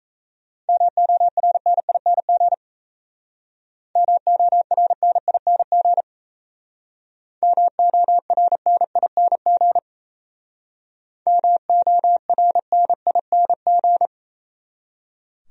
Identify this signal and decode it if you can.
Morse code, heard as a keyed tone, about 34 wpm, 700 Hz. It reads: MORNING MORNING MORNING MORNING
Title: Speed-Racing Top 500 Words 28wpm